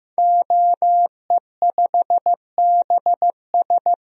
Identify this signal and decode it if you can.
Morse code, heard as a keyed tone, about 15 wpm, 700 Hz. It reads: OE5BS
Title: DXCC-100 Call Signs - 5 Characters - Repeated 3x - 15wpm